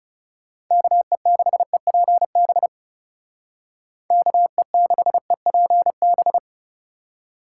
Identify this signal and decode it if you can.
Morse code, heard as a keyed tone, about 37 wpm, 700 Hz. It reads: KE6EPB KE6EPB
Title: Speed Racing - USA 2x3 Call Signs 30wpm